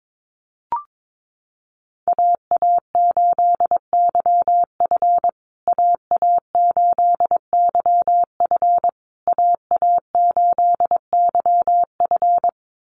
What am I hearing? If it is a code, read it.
Morse code, heard as a keyed tone, about 22 wpm, 700 Hz. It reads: AA8YF AA8YF AA8YF
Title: USA Call Signs Repeated 3x 22wpm